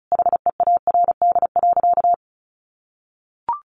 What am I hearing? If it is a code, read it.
Morse code, heard as a keyed tone, about 34 wpm, 700 Hz. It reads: HEARD.
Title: Sentences from Top 400 Words 35wpm